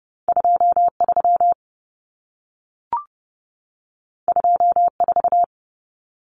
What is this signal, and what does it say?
Morse code, heard as a keyed tone, about 30 wpm, 700 Hz. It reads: 23 24
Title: Common Cut Numbers - 2-Digit 30wpm